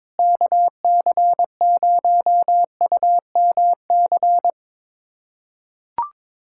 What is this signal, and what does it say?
Morse code, heard as a keyed tone, about 22 wpm, 700 Hz. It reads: KC0UMC
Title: USA Callsign DE Callsign - 6 Characters - 22wpm